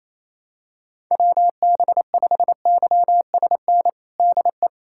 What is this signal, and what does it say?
Morse code, heard as a keyed tone, about 28 wpm, 700 Hz. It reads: WB5YSN DE
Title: USA Callsign DE Callsign - 6 Characters - 28wpm